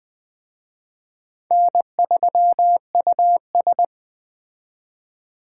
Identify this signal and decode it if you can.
Morse code, heard as a keyed tone, about 20 wpm, 700 Hz. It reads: N3US